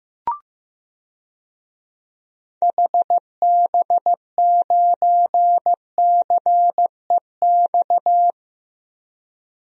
Morse code, heard as a keyed tone, about 15 wpm, 700 Hz. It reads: HB9CEX
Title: DXCC-100 Call Signs - 6 Characters - 15wpm